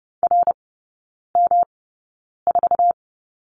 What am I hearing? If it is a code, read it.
Morse code, heard as a keyed tone, about 30 wpm, spaced as for 9 wpm, 700 Hz. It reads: RM4
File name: Lesson 18e - Letter m - Three Characters 30wpm